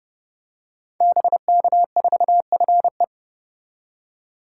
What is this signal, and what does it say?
Morse code, heard as a keyed tone, about 30 wpm, 700 Hz. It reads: BK4FE
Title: Code Groups - 5 Letter:Numbers - 30wpm